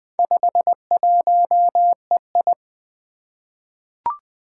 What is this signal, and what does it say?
Morse code, heard as a keyed tone, about 20 wpm, 700 Hz. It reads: 51EI
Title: Four Challenging Letter-Number Combinations 20wpm